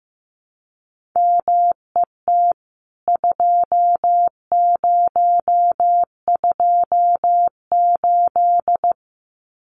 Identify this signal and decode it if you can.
Morse code, heard as a keyed tone, about 15 wpm, 700 Hz. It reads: MET 2028
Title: CWT Exchange 15wpm